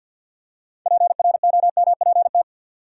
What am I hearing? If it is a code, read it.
Morse code, heard as a keyed tone, about 47 wpm, 700 Hz. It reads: PROMPT
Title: Six Letter Words - 50wpm